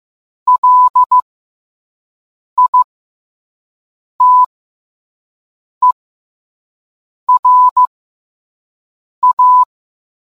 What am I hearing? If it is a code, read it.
Morse code, heard as a keyed tone, about 15 wpm, spaced as for 5 wpm, 1000 Hz. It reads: LITERA